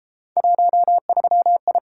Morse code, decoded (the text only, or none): 13I